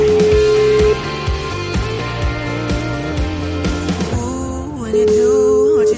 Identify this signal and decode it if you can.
Ringback tone (Europe, 425 Hz), with noise.